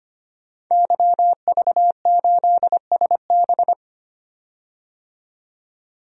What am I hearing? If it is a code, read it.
Morse code, heard as a keyed tone, about 25 wpm, 700 Hz. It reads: YV8SB
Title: DXCC-100 Call Signs - 5 Characters - 25wpm